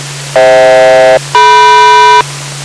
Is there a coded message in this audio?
Yes: aircraft SELCAL from 0.4 to 2.2 s, heard as two pairs of tones. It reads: FH-DM